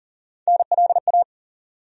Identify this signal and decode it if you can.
Morse code, heard as a keyed tone, about 40 wpm, 700 Hz. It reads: NLA
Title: USA Call Sign Postfixes - 3 Letters 40wpm